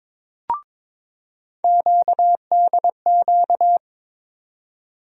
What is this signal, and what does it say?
Morse code, heard as a keyed tone, about 23 wpm, 700 Hz. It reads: QDQ
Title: USA Call Sign Postfixes - 3 Letters 22wpm